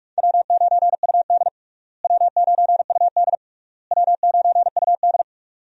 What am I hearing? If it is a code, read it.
Morse code, heard as a keyed tone, about 42 wpm, 700 Hz. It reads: W9UD W9UD W9UD
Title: USA Call Signs - 4 Characters - Repeated 3x - 45wpm